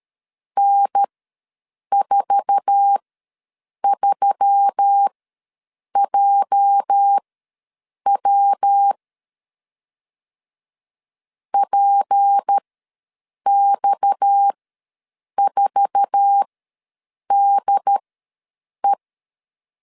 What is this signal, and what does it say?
Morse code, heard as a keyed tone, about 13 wpm, spaced as for 7 wpm, 800 Hz. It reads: N43JW PX4DE
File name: LEZIONI / 8 YJ45 / 8 YJ4 ws 12wpm 3249